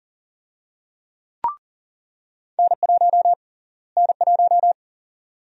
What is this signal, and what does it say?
Morse code, heard as a keyed tone, about 40 wpm, 700 Hz. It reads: N1 N1
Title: USA Call Sign Prefixes - Repeated 2x - 40wpm